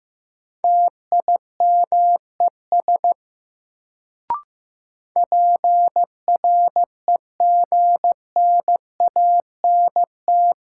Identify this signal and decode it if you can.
Morse code, heard as a keyed tone, about 15 wpm, 700 Hz. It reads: TIMES PREGNANT
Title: Sentences from Top 1000 Words - Review Part 6 - New Word Drill - 15wpm